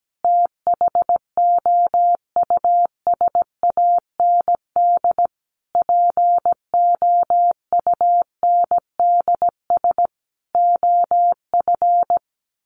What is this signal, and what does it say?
Morse code, heard as a keyed tone, about 17 wpm, 700 Hz. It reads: THOUSAND POUNDS OF
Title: A Sound Of Thunder 17wpm